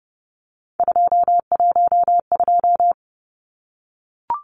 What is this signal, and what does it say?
Morse code, heard as a keyed tone, about 30 wpm, 700 Hz. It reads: 212
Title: Numbers - 3-Digit 30wpm